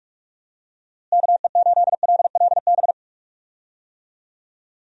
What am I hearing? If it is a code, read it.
Morse code, heard as a keyed tone, about 44 wpm, 700 Hz. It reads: KE8LLB
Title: USA Call Signs - 6 Characters - Repeated 2x - 45wpm